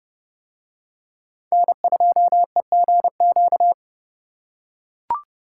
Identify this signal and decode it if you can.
Morse code, heard as a keyed tone, about 29 wpm, 700 Hz. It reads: N2EGQ